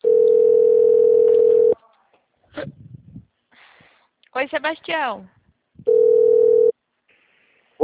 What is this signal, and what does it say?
Ringback tone (North America), 440 + 480 Hz, with speech.
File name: out- -2002- - - .80785